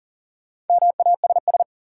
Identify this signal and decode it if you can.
Morse code, heard as a keyed tone, about 40 wpm, 700 Hz. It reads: MASS
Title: Sets of 5 Words Encore - New Word Drill - 40wpm